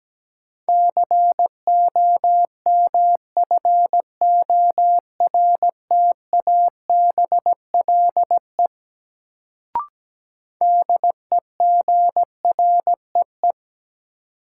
Morse code, heard as a keyed tone, about 17 wpm, 700 Hz. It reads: COMFORTABLE DEGREE
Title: Sentences from Top 1500 Words - New Word Drill Alt - 17wpm